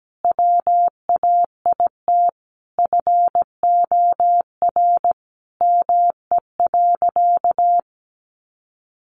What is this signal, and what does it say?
Morse code, heard as a keyed tone, about 17 wpm, 700 Hz. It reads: WAIT FOR ME.
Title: Sentences from Top 600 Words 17wpm